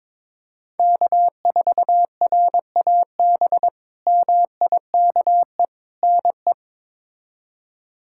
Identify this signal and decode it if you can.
Morse code, heard as a keyed tone, about 21 wpm, 700 Hz. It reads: K4RAB MIKE NE
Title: K1USN SST QSO Practice - 22wpm